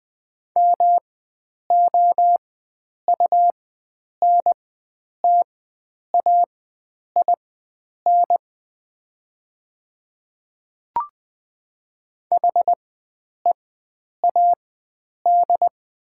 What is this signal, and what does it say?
Morse code, heard as a keyed tone, about 20 wpm, spaced as for 9 wpm, 700 Hz. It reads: MOUNTAIN HEAD
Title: 4x Character Spacing - Top 300 Words - 20wpm